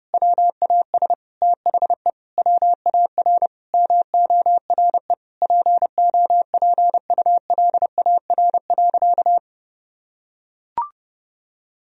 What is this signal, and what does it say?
Morse code, heard as a keyed tone, about 32 wpm, 700 Hz. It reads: WAS THE WAR MORE POPULAR.
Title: Sentences from Top 1000 Words - Review Part 3 - 30wpm